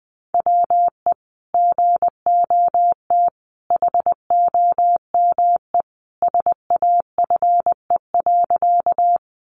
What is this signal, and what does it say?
Morse code, heard as a keyed tone, about 20 wpm, 700 Hz. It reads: WE GOT HOME SAFE.
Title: Sentences from Top 900 Words 20wpm